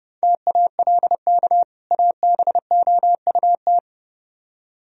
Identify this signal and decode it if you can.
Morse code, heard as a keyed tone, about 37 wpm, 700 Hz. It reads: TALK ABOUT
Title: Speed-Racing - Sets of 2 Words 30wpm